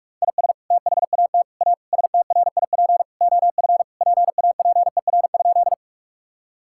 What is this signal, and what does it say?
Morse code, heard as a keyed tone, about 43 wpm, 700 Hz. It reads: IS THAT A STRIP OF PAPER?
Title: Sentences from Top 1600 Words 45wpm